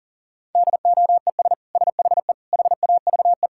Morse code, heard as a keyed tone, about 37 wpm, 700 Hz. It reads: DOES SHE HAVE
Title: Sentences from Top 1000 Words - Review Part 6 - 40wpm